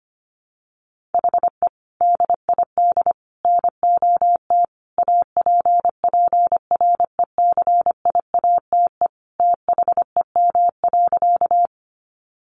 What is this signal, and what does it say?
Morse code, heard as a keyed tone, about 25 wpm, 700 Hz. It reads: HE DID NOT APPRECIATE THEM.
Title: Sentences from Top 1300 Words 25wpm